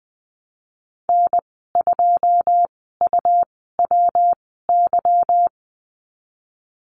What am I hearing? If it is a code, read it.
Morse code, heard as a keyed tone, about 20 wpm, spaced as for 15 wpm, 700 Hz. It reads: N2UWY
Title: 167 Lesson 27g - Number 7 - Call Signs 20wpm